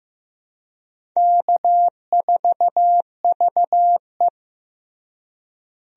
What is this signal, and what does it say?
Morse code, heard as a keyed tone, about 16 wpm, 700 Hz. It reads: K4VE
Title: USA Call Signs - 4 Characters - Repeated 2x - 15wpm